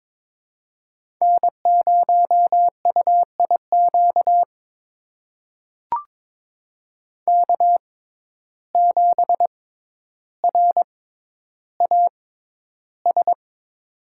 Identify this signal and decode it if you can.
Morse code, heard as a keyed tone, about 22 wpm, spaced as for 11 wpm, 700 Hz. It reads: N0UIQ K7RAS
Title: USA Call Signs - 5 Characters - 6x Character Spacing - 22wpm